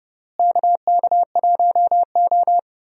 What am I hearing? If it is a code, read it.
Morse code, heard as a keyed tone, about 30 wpm, 700 Hz. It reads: KK1O
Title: USA Call Signs - 4 Characters - Repeated 3x - 30wpm